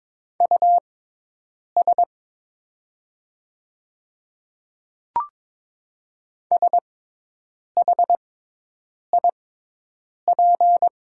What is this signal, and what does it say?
Morse code, heard as a keyed tone, about 22 wpm, spaced as for 8 wpm, 700 Hz. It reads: US SHIP
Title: 6x Character Spacing - Top 500 Words - 22wpm